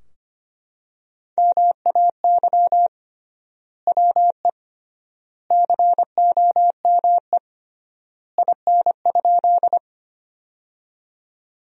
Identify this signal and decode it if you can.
Morse code, heard as a keyed tone, about 25 wpm, 700 Hz. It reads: MAY WE COME IN?
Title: 3x Word-Spacing - Sentences from Top 100 Words - 25wpm